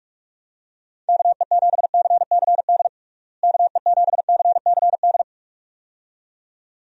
Morse code, heard as a keyed tone, about 46 wpm, 700 Hz. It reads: KE7CCD KE7CCD
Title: USA Call Signs - 6 Characters - Repeated 2x - 45wpm